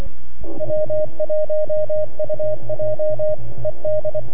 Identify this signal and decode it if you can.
Morse code, heard as a keyed tone, about 24 wpm, 610 Hz, with radio noise. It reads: W1UJ ED
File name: NAQPCW-8-2009 / W1UJ NAQP CW 09 0100Z